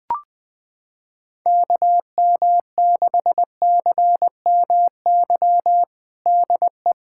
Morse code, heard as a keyed tone, about 20 wpm, 700 Hz. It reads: KM6CMY DE
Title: USA Callsign DE Callsign - 6 Characters - 20wpm